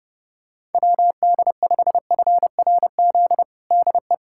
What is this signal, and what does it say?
Morse code, heard as a keyed tone, about 30 wpm, 700 Hz. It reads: WD5FRZ DE
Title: USA Callsign DE Callsign 30wpm